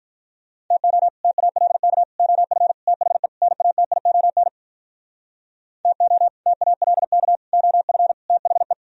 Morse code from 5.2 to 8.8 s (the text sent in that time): TO TALK OF THE